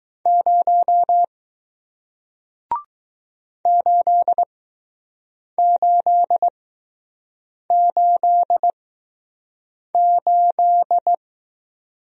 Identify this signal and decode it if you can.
Morse code, heard as a keyed tone, about 20 wpm, spaced as for 6 wpm, 700 Hz. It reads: 0 8888